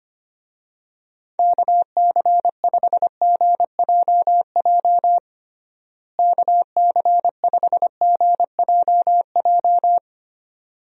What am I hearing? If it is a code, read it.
Morse code, heard as a keyed tone, about 25 wpm, 700 Hz. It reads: KC5GJJ KC5GJJ
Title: USA Call Signs - 6 Characters - Repeated 2x - 25wpm